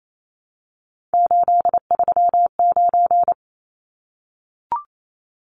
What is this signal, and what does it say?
Morse code, heard as a keyed tone, about 28 wpm, 700 Hz. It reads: 839
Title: Numbers - 3-Digit 28wpm